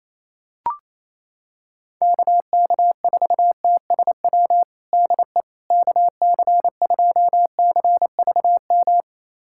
Morse code, heard as a keyed tone, about 28 wpm, 700 Hz. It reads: KK4TSW DE KC2CVM
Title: USA Callsign DE Callsign - 6 Characters - 28wpm